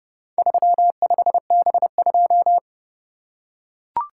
Morse code, heard as a keyed tone, about 29 wpm, 700 Hz. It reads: 35B2